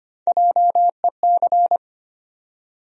Morse code, heard as a keyed tone, about 25 wpm, 700 Hz. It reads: JEC
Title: USA Call Sign Postfixes - 3 Letters 25wpm